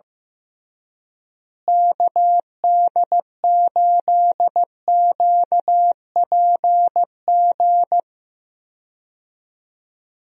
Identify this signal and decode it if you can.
Morse code, heard as a keyed tone, about 15 wpm, 700 Hz. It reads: KD8QPG
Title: USA Call Signs - 6 Characters - Repeated 3x - 15wpm